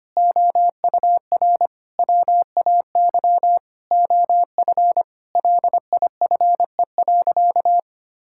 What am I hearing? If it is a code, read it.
Morse code, heard as a keyed tone, about 25 wpm, 700 Hz. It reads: OUR WAY OF LIFE.
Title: Sentences from Top 1000 Words - Review Part 6 - 25wpm